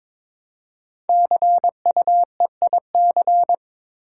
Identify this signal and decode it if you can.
Morse code, heard as a keyed tone, about 22 wpm, 700 Hz. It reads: CUEIC
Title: Code Groups - 5 Letter:Numbers - 22wpm